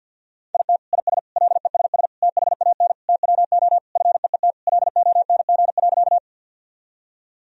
Morse code, heard as a keyed tone, about 50 wpm, 700 Hz. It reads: IT IS LESS THAN TWO FEET LONG.
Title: Sentences from Top 1000 Words - Review Part 2 - 50wpm